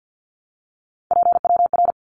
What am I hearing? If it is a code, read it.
Morse code, heard as a keyed tone, about 50 wpm, 700 Hz. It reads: PWR
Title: Abbreviations Repeated 3x - 50wpm